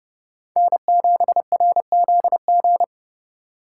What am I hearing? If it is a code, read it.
Morse code, heard as a keyed tone, about 39 wpm, 700 Hz. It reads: N7RZG